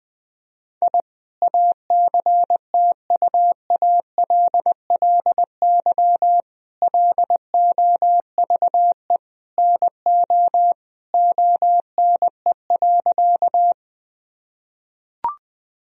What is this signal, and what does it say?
Morse code, heard as a keyed tone, about 19 wpm, 700 Hz. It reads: I ACTUALLY LOVE NO ONE.
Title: Sentences from Top 1000 Words - Review Part 4 - 20wpm